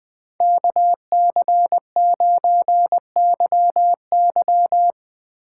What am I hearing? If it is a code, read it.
Morse code, heard as a keyed tone, about 22 wpm, 700 Hz. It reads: KC9YY
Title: Speed Racing - USA 2x2 Call Signs 20wpm